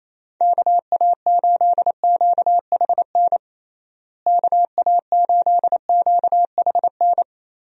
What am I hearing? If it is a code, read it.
Morse code, heard as a keyed tone, about 28 wpm, 700 Hz. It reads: KA8QHN KA8QHN
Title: USA Call Signs - 6 Characters - Repeated 3x - 28wpm